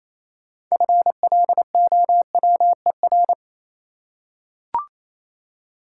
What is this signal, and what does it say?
Morse code, heard as a keyed tone, about 28 wpm, 700 Hz. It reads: FLOWER
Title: Six Letter Words - 28wpm